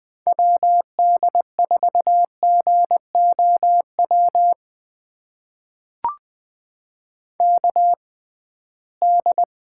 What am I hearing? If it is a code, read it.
Morse code, heard as a keyed tone, about 20 wpm, spaced as for 15 wpm, 700 Hz. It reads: WD4GOW KD
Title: USA Call Signs - 6x Character Spacing - 20wpm